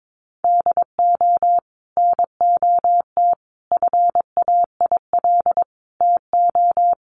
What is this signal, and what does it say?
Morse code, heard as a keyed tone, about 22 wpm, 700 Hz. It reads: DO NOT FAIL TO
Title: Sentences from Top 1100 Words 22wpm